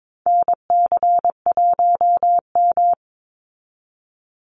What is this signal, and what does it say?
Morse code, heard as a keyed tone, about 22 wpm, 700 Hz. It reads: NC1M
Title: Call Signs Repeated 2x - 22wpm